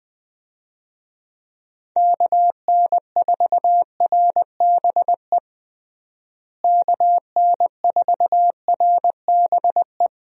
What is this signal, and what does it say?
Morse code, heard as a keyed tone, about 21 wpm, 700 Hz. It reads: KN4RBE KN4RBE
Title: USA Call Signs - 6 Characters - Repeated 3x - 20wpm